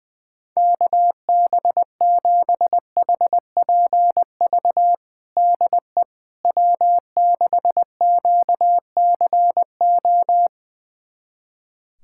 Morse code, heard as a keyed tone, about 20 wpm, 700 Hz. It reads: KB7HPV DE W6QCO
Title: USA Callsign DE Callsign 20wpm